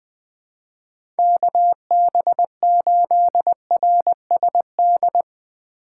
Morse code, heard as a keyed tone, about 20 wpm, 700 Hz. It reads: KB8RSD
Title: USA Call Signs - 6 Characters - 3x Character Spacing - 20wpm